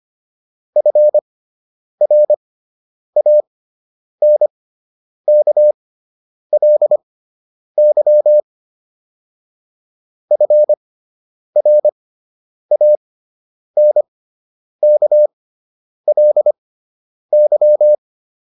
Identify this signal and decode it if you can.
Morse code, heard as a keyed tone, about 25 wpm, spaced as for 9 wpm, 600 Hz. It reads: FRANKLY FRANKLY
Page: fundamental-practice-files